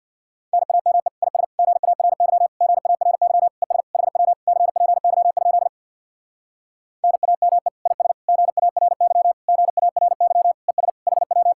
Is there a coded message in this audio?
Yes: Morse code from 0.5 to 11.5 s, heard as a keyed tone, about 66 wpm, 700 Hz. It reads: NAME IS GARY GARY ES HW CPY? NAME IS GARY GARY ES HW